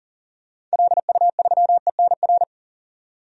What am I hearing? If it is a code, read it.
Morse code, heard as a keyed tone, about 40 wpm, 700 Hz. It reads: LU3ENR
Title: DXCC-100 Call Signs - 6 Characters - 40wpm